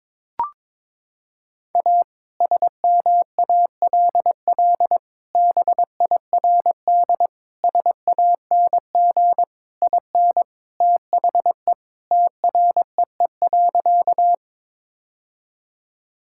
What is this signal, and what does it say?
Morse code, heard as a keyed tone, about 22 wpm, 700 Hz. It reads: A SMALL BIRD SANG IN THE TREE.
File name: practice / 22wpm / Sentences with 2-Word Phrases - 22wpm